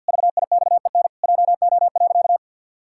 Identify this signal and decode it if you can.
Morse code, heard as a keyed tone, about 46 wpm, 700 Hz. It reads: VIXEN JO.
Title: Pangram Challenge v2 - 50wpm